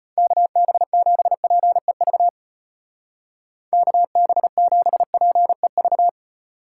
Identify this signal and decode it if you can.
Morse code, heard as a keyed tone, about 33 wpm, 700 Hz. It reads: KB7PEV KB7PEV